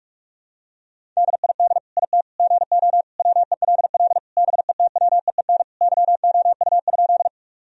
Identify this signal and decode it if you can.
Morse code, heard as a keyed tone, about 46 wpm, 700 Hz. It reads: DID IT GO WELL BETWEEN YOU?
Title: Sentences from Top 1000 Words - Review Part 1 - 45wpm